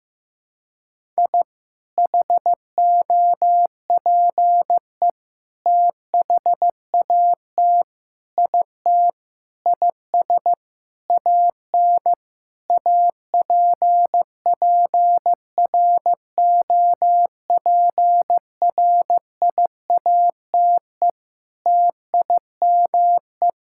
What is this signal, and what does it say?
Morse code, heard as a keyed tone, about 15 wpm, 700 Hz. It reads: I HOPE THAT IT IS AN APPROPRIATE TIME
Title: Sentences from Top 1000 Words - Review Part 5 - 15wpm